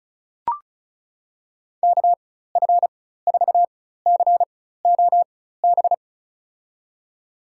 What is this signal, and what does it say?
Morse code, heard as a keyed tone, about 36 wpm, spaced as for 16 wpm, 700 Hz. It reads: KF4COB
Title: USA Call Signs - 4x Character Spacing - 35wpm